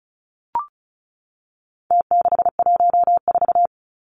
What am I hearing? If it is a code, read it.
Morse code, heard as a keyed tone, about 34 wpm, 700 Hz. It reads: T614
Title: Numbers - Contest Style - 4-Digit 35wpm